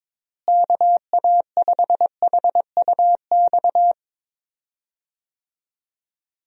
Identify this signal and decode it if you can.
Morse code, heard as a keyed tone, about 22 wpm, 700 Hz. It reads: KA5HUX
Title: USA Call Signs - 6 Characters - Repeated 2x - 22wpm